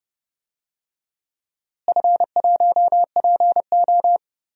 Morse code, heard as a keyed tone, about 31 wpm, 700 Hz. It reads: F1PO
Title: DXCC-100 Call Signs - 4 Characters - Repeated 2x - 30wpm